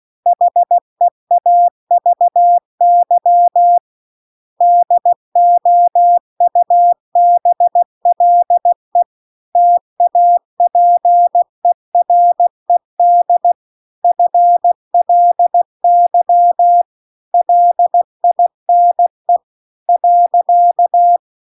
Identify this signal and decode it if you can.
Morse code, heard as a keyed tone, about 16 wpm, 700 Hz. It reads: HEAVY DOUBLE TAPERED FLY LINE .